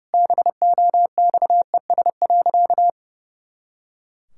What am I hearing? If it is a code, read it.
Morse code, heard as a keyed tone, about 30 wpm, 700 Hz. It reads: BOXES.
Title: Pangram Challenge v2 - 30wpm